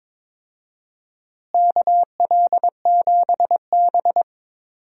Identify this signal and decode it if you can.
Morse code, heard as a keyed tone, about 22 wpm, 700 Hz. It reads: KL7B
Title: DXCC-100 Call Signs - 4 Characters - Repeated 2x - 22wpm